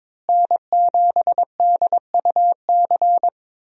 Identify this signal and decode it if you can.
Morse code, heard as a keyed tone, about 22 wpm, 700 Hz. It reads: N7DUC
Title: USA Call Signs - 5 Characters - Repeated 3x - 22wpm